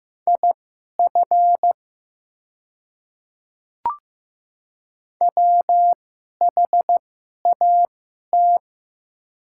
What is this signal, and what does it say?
Morse code, heard as a keyed tone, about 15 wpm, spaced as for 11 wpm, 700 Hz. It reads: IF WHAT
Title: 2x Character Spacing - Top 100 Words - 15wpm